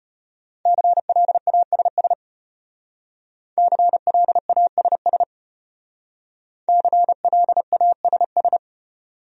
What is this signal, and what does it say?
Morse code, heard as a keyed tone, about 32 wpm, 700 Hz. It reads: CLASS CLASS CLASS